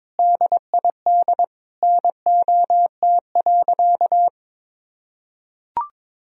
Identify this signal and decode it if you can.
Morse code, heard as a keyed tone, about 22 wpm, 700 Hz. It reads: DID NOT.
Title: Sentences from Top 1000 Words - Review Part 1 - 22wpm